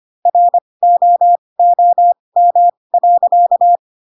Morse code, heard as a keyed tone, about 25 wpm, spaced as for 20 wpm, 700 Hz. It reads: ROOM.
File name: Pinocchio / Pinocchio ch7 15 25 ik1hge